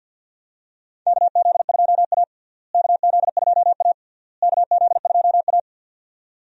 Morse code, heard as a keyed tone, about 52 wpm, 700 Hz. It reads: KZ2A KZ2A KZ2A
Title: USA Call Signs - 4 Characters - Repeated 3x - 50wpm